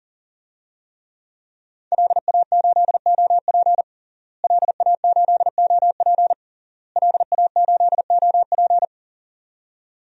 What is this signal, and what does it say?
Morse code, heard as a keyed tone, about 40 wpm, 700 Hz. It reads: LA8OP LA8OP LA8OP
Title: DXCC-100 Call Signs - 5 Characters - Repeated 3x - 40wpm